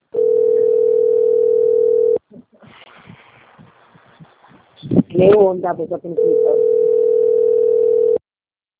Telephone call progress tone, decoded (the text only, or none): ringback tone